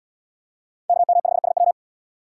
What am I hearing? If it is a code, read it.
Morse code, heard as a keyed tone, about 70 wpm, 700 Hz. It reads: KM5SW